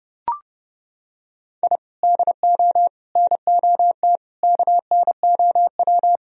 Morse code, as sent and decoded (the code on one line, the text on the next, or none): .. / -.. --- / -. --- - / -.- -. --- .--
I DO NOT KNOW